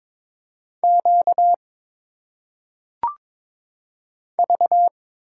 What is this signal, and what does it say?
Morse code, heard as a keyed tone, about 22 wpm, 700 Hz. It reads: Q V
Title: Single Letters - Phonetic Pronunciation 22wpm